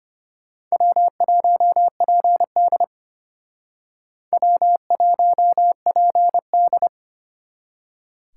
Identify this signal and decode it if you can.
Morse code, heard as a keyed tone, about 32 wpm, 700 Hz. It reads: W1PD W1PD